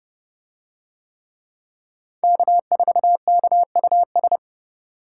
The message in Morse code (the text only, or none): K4KUS